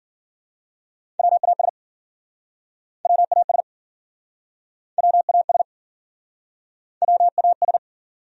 Morse code, heard as a keyed tone, about 52 wpm, 700 Hz. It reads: WAS WAS WAS WAS